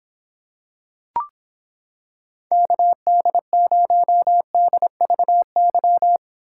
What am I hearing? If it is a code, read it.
Morse code, heard as a keyed tone, about 26 wpm, 700 Hz. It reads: KD0DVY